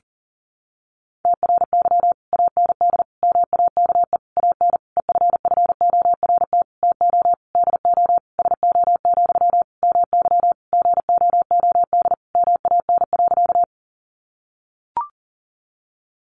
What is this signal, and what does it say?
Morse code, heard as a keyed tone, about 40 wpm, 700 Hz. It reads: TRY AND MAKE AN EFFORT TO DO SO, MY GOOD MAN.
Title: Sentences from Top 1200 Words 40wpm